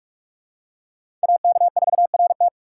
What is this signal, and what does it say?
Morse code, heard as a keyed tone, about 43 wpm, 700 Hz. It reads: AK4RT
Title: USA Call Signs - 5 Characters - 2x Character Spacing - 45wpm